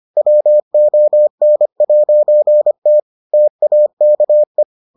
Morse code, heard as a keyed tone, about 25 wpm, 600 Hz. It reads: WON'T TAKE